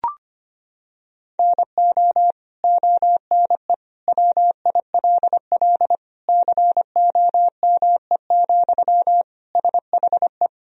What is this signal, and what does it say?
Morse code, heard as a keyed tone, about 25 wpm, 700 Hz. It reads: NO ONE WILL COME, SHE
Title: Sentences from Top 100 Words and Abbreviations 25wpm